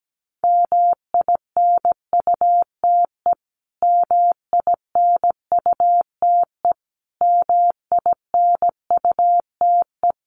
Morse code, heard as a keyed tone, about 17 wpm, 700 Hz. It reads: MINUTE MINUTE MINUTE
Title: Top 500 Words Repeated 17wpm